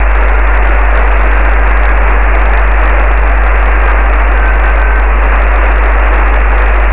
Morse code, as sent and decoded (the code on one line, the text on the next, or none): none